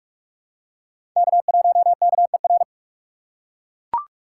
Morse code, heard as a keyed tone, about 44 wpm, 690 Hz. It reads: K1KER
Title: USA Call Signs - 5 Characters - 8x Character Spacing - 45wpm